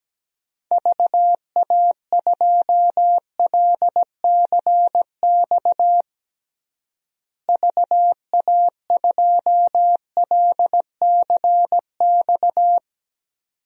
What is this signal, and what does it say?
Morse code, heard as a keyed tone, about 17 wpm, 700 Hz. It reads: VA2LCX VA2LCX
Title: DXCC-100 Call Signs - 6 Characters - Repeated 2x - 17wpm